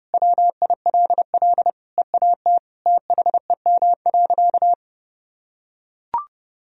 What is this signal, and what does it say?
Morse code, heard as a keyed tone, about 32 wpm, 700 Hz. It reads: WILL EAT THEM.
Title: Sentences from Top 1000 Words - Review Part 4 - 30wpm